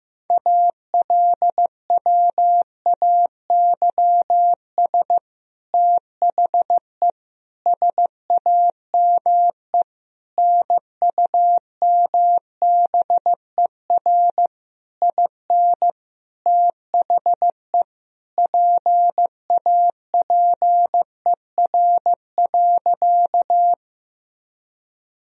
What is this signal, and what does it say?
Morse code, heard as a keyed tone, about 16 wpm, 700 Hz. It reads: ALWAYS THE SAME NUMBER IN THE PAPER.
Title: Sentences from Top 1000 Words - Review Part 1 - 15wpm